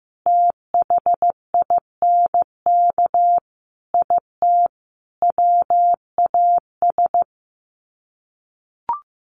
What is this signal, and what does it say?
Morse code, heard as a keyed tone, about 15 wpm, 700 Hz. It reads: THINK IT WAS
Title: Sets of 5 Words Encore 15wpm